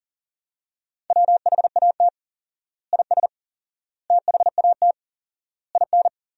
Morse code, heard as a keyed tone, about 40 wpm, 700 Hz. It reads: WHAT IS THAT IN
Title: 4x Word-Spacing - Sentences from Top 200 Words - 40wpm